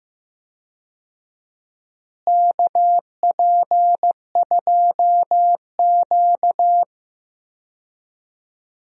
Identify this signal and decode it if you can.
Morse code, heard as a keyed tone, about 15 wpm, 700 Hz. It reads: KP2Q
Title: DXCC-100 Call Signs - 4 Characters - Repeated 2x - 15wpm